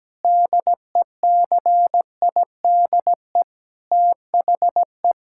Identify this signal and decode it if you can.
Morse code, heard as a keyed tone, about 17 wpm, 700 Hz. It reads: DECIDE THE
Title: Sentences from Top 1600 Words 17wpm